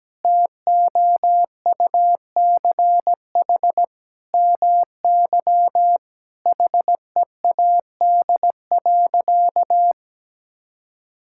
Morse code, heard as a keyed tone, about 17 wpm, 700 Hz. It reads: TOUCH MY HEAD.
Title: Sentences from Top 1600 Words 17wpm